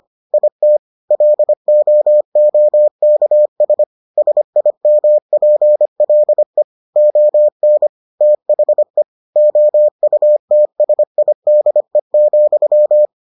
Morse code, heard as a keyed tone, about 25 wpm, 600 Hz. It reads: IT LOOKS SIMPLE ON THE OUTSIDE,